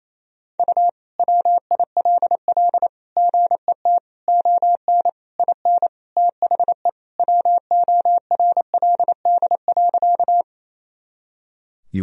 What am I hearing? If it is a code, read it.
Morse code, heard as a keyed tone, about 28 wpm, 700 Hz. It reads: U WILL GET ON IN THE WORLD.
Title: Sentences from Top 200 Words and Abbreviations 28wpm